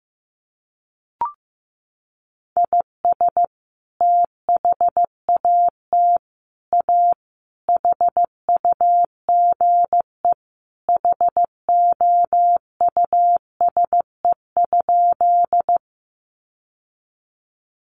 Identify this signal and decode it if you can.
Morse code, heard as a keyed tone, about 15 wpm, 700 Hz. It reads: IS THAT A HUGE HOUSE?
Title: Sentences from Top 1100 Words 15wpm